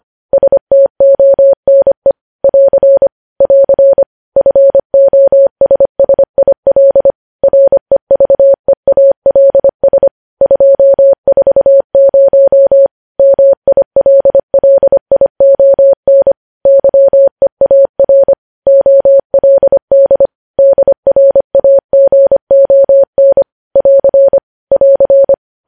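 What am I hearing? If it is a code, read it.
Morse code, heard as a keyed tone, about 25 wpm, 550 Hz. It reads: STONE <AR> <AR> FOSSIL REVEALS 240 MILLION YEAR OLD DRAGON <AR> <AR>
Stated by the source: Morse code news headlines sent at 25 wpm.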